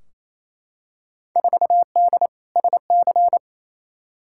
Morse code, heard as a keyed tone, about 28 wpm, 700 Hz. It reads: 4D SC
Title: Field Day Exchanges Encore - 28wpm